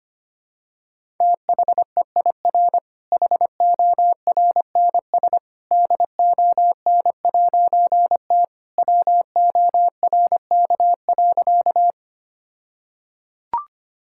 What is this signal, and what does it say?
Morse code, heard as a keyed tone, about 25 wpm, 700 Hz. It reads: THEIR HORNS DON'T WORK.
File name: practice / 25wpm / All-Age Appropriate Jokes - 25wpm